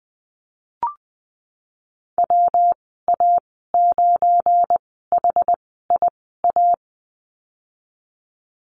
Morse code, heard as a keyed tone, about 19 wpm, spaced as for 14 wpm, 700 Hz. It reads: WA9HIA
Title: 179 Lesson 29g - Slash - Call Signs 20wpm